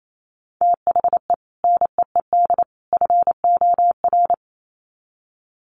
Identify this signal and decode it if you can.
Morse code, heard as a keyed tone, about 28 wpm, 700 Hz. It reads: THE NEED FOR
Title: Sets of 3 Words 28wpm